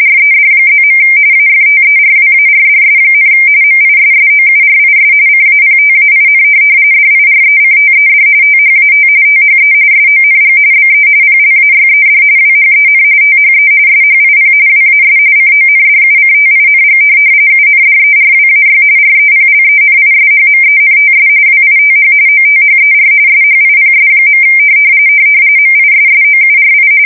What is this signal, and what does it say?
RTTY, about 45.45 baud. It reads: RED+. AFTERWARDS, MISSOURIS CAPTAIN DIRECTED THAT THE SONG BE PLAYED WHEN THE SHIP WAS CONDUCTING UNDERWAY REPLENISHMENTS WITH OTHER SHIPS. MISSOURI
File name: fsk170 / USS-Missouri